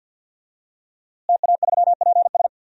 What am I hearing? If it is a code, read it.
Morse code, heard as a keyed tone, about 51 wpm, 700 Hz. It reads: TA3PS
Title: DXCC-100 Call Signs - 5 Characters - Repeated 3x - 50wpm